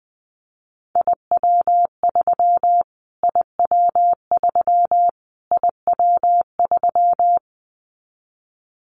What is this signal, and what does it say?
Morse code, heard as a keyed tone, about 20 wpm, 700 Hz. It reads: IW3 IW3 IW3
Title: Callsign Prefixes for Top 25 DXCC - Repeated 3x - 20wpm